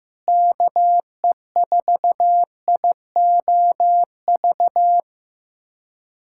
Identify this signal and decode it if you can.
Morse code, heard as a keyed tone, about 15 wpm, 700 Hz. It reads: KE4IOV
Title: USA Callsign DE Callsign - 6 Characters - 15wpm